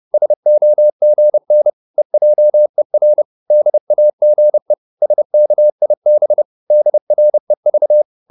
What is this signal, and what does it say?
Morse code, heard as a keyed tone, about 30 wpm, 600 Hz. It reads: SOGN EJER DAGE SKIB DREV
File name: Korte ord DK 30wpm